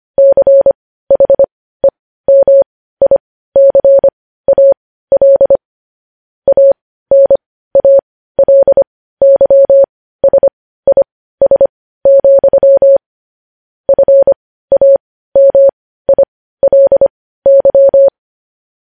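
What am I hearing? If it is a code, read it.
Morse code, heard as a keyed tone, about 25 wpm, spaced as for 15 wpm, 550 Hz. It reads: CHEMICAL ANALYSIS, FAMILY
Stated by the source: Morse code news headlines sent at 15 words per minute.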